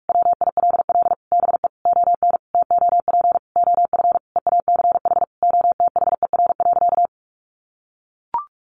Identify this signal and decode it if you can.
Morse code, heard as a keyed tone, about 44 wpm, 700 Hz. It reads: WILL BE ON TOP OF EACH OTHER.
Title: Sentences from Top 700 Words 45wpm